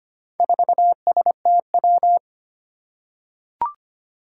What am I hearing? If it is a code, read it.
Morse code, heard as a keyed tone, about 25 wpm, 700 Hz. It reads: 4STW
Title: Four Challenging Letter-Number Combinations 25wpm